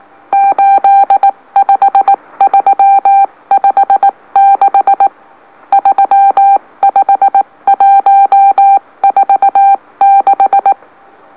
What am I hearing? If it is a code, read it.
Morse code, heard as a keyed tone, about 19 wpm, spaced as for 16 wpm, 790 Hz. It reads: 85356 35146
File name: M04- -2345utc-LOLO